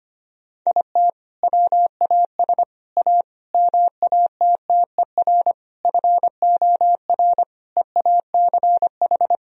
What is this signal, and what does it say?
Morse code, heard as a keyed tone, about 25 wpm, 690 Hz. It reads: IT WAS A MATTER FOR EACH
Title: Sentences from Top 1000 Words - Review Part 4 - 25wpm